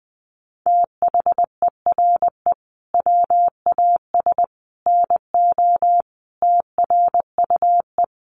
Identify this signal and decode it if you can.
Morse code, heard as a keyed tone, about 20 wpm, 700 Hz. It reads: THERE WAS NO TRUE
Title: Sentences from Top 1300 Words 20wpm